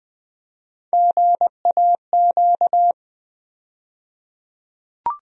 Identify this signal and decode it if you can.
Morse code, heard as a keyed tone, about 20 wpm, 700 Hz. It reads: GAQ
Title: USA Call Sign Postfixes - 3 Letters 20wpm